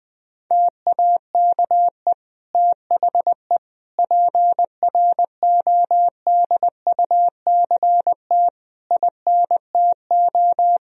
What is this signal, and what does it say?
Morse code, heard as a keyed tone, about 20 wpm, 700 Hz. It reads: TAKE THE PRODUCT INTO
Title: Sentences from Top 1000 Words - Review Part 2 - 20wpm